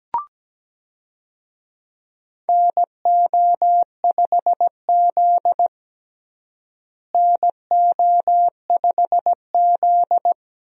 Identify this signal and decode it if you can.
Morse code, heard as a keyed tone, about 17 wpm, 700 Hz. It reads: NO5Z NO5Z
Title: USA Call Signs - 4 Characters - Repeated 3x - 17wpm